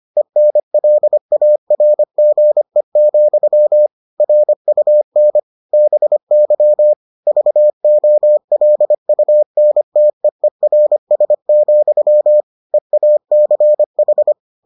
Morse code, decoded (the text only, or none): ENLARGE, RUN BY VOLUNTEERS, EACH